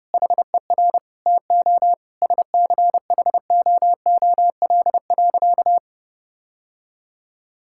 Morse code, heard as a keyed tone, about 30 wpm, 700 Hz. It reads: HER TO SCHOOL.
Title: Sentences from Top 1000 Words - Review Part 5 - 30wpm